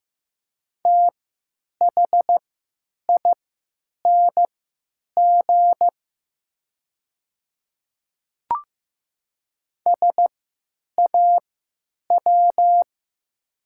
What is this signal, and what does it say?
Morse code, heard as a keyed tone, about 15 wpm, spaced as for 9 wpm, 700 Hz. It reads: THING SAW
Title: 3x Character Spacing - Top 300 Words - 15wpm